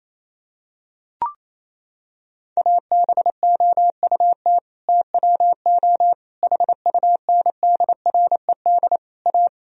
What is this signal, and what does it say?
Morse code, heard as a keyed tone, about 28 wpm, 700 Hz. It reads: ABOUT TWO HUNDRED A
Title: Sentences from Top 1000 Words - Review Part 2 - 28wpm